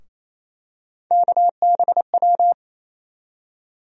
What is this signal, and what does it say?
Morse code, heard as a keyed tone, about 28 wpm, 700 Hz. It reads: KBW
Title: USA Call Sign Postfixes - 3 Letters 28wpm